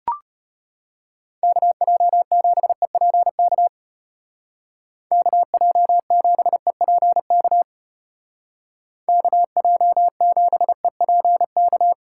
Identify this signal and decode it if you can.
Morse code, heard as a keyed tone, about 32 wpm, 700 Hz. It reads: KJ7EPK KJ7EPK KJ7EPK